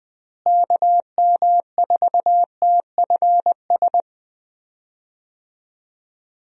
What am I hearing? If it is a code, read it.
Morse code, heard as a keyed tone, about 20 wpm, 700 Hz. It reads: KM4TFS
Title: USA Call Signs - 6 Characters - Repeated 3x - 20wpm